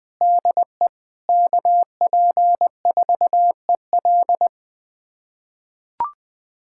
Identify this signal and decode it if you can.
Morse code, heard as a keyed tone, about 20 wpm, 700 Hz. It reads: DE KP4EL
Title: USA Callsign DE Callsign 20wpm